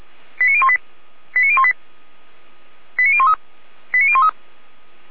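Five-tone selective call (ZVEI), heard as quick runs of five tones, tones of about 70 ms. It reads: 89018 89018 89013 89013
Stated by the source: ZVEI five-tone selective call received by SDR, from feuerwehr-orlamuende fire service.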